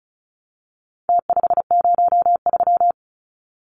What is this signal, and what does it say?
Morse code, heard as a keyed tone, about 34 wpm, 700 Hz. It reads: T503
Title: Numbers - Contest Style - 4-Digit 35wpm